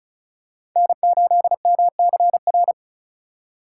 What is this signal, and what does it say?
Morse code, heard as a keyed tone, about 36 wpm, 700 Hz. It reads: N8MCR
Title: USA Call Signs - 4x Character Spacing - 35wpm